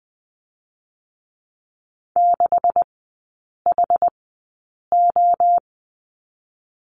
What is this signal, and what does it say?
Morse code, heard as a keyed tone, about 20 wpm, spaced as for 8 wpm, 700 Hz. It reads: 6HO
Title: 127 Lesson 21e - Number 6 - Three Characters 20wpm